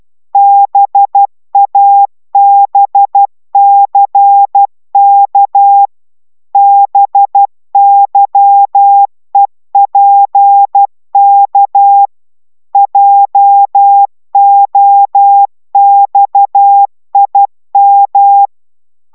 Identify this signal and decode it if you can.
Morse code, heard as a keyed tone, about 12 wpm, 800 Hz. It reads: BABCK BYEPK JOXIM